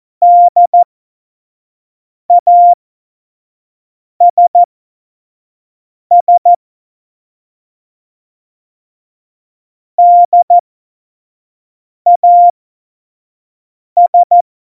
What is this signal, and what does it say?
Morse code, heard as a keyed tone, about 14 wpm, spaced as for 5 wpm, 700 Hz. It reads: DASS DAS
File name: Morsekurs / Witze / Witz 27. Oktober 2020, 25 F70